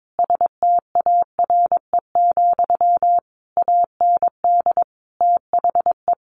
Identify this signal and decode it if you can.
Morse code, heard as a keyed tone, about 22 wpm, 700 Hz. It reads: STARE, AND THE
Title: A Sound Of Thunder 22wpm